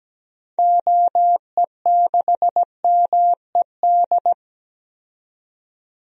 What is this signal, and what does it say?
Morse code, heard as a keyed tone, about 17 wpm, 700 Hz. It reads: OE6MED
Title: DXCC-100 Call Signs - 6 Characters - 17wpm